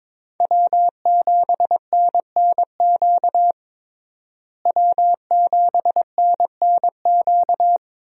Morse code, heard as a keyed tone, about 22 wpm, 700 Hz. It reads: W7NNQ W7NNQ
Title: USA Call Signs - 5 Characters - Repeated 2x - 22wpm